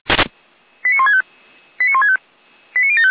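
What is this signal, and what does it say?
Five-tone selective call (ZVEI), heard as quick runs of five tones, tones of about 70 ms. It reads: 89165 89165 89075
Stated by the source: Received on an SDR.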